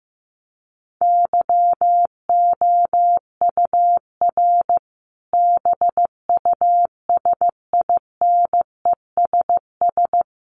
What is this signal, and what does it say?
Morse code, heard as a keyed tone, about 15 wpm, 700 Hz. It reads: YOUR BUSINESS
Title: Sentences from Top 1200 Words 15wpm